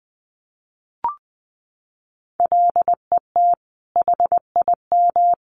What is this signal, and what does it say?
Morse code, heard as a keyed tone, about 20 wpm, 700 Hz. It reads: LET HIM
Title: Sentences from Top 1400 Words 20wpm